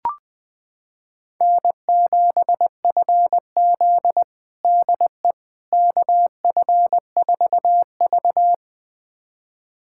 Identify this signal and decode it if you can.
Morse code, heard as a keyed tone, about 20 wpm, 700 Hz. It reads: N7FZ DE KF4V
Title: USA Callsign DE Callsign - 4 Characters - 20wpm